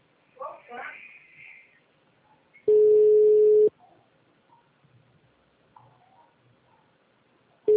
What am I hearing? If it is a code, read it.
Ringback tone (Europe, 425 Hz), with speech.
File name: out- -2007- - - .2